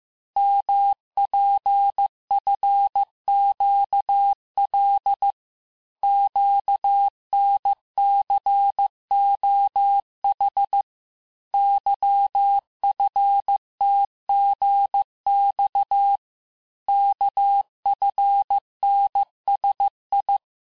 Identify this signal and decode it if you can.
Morse code, heard as a keyed tone, about 15 wpm, 800 Hz. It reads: MPFQL QNCOH YFTGX KFNSI
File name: cours-du-lundi files / cours-1585-texte-chiffre-vitesse-14mpm